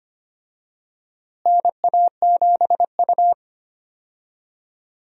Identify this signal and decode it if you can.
Morse code, heard as a keyed tone, about 25 wpm, 700 Hz. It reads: NA7U
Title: USA Call Signs - 4 Characters - Repeated 2x - 25wpm